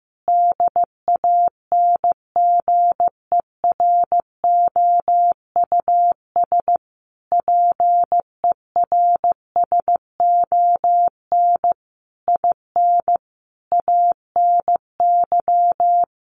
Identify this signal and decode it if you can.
Morse code, heard as a keyed tone, about 15 wpm, 700 Hz. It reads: DANGEROUS PERSON IN ANY
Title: Sentences from Top 1200 Words 15wpm